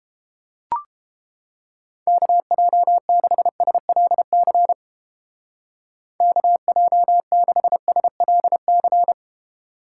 Morse code, heard as a keyed tone, about 29 wpm, 700 Hz. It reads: KJ6SLC KJ6SLC